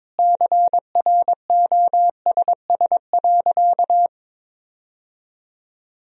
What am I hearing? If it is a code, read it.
Morse code, heard as a keyed tone, about 22 wpm, 700 Hz. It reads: CROSS.
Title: Sentences from Top 1000 Words - Review Part 2 - 22wpm